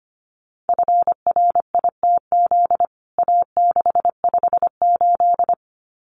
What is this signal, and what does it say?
Morse code, heard as a keyed tone, about 25 wpm, 700 Hz. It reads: FRITZ A658
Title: CWT Exchange 25wpm